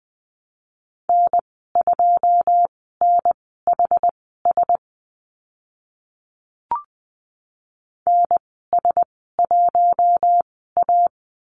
Morse code, heard as a keyed tone, about 20 wpm, spaced as for 14 wpm, 700 Hz. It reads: N2NHS NS1A
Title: 082 Lesson 14g - Number 2 - Call Signs 20wpm